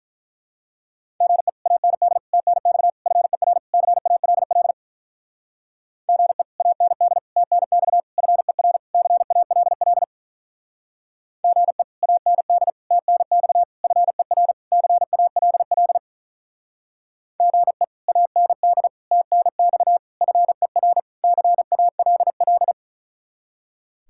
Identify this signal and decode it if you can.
Morse code, heard as a keyed tone, about 45 wpm, 700 Hz. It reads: GE AND TNX FER CALL GE AND TNX FER CALL GE AND TNX FER CALL GE AND TNX FER CALL